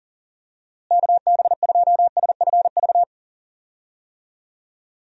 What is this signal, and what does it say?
Morse code, heard as a keyed tone, about 40 wpm, 700 Hz. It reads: KB2SFV
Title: USA Call Signs - 6 Characters - Repeated 2x - 40wpm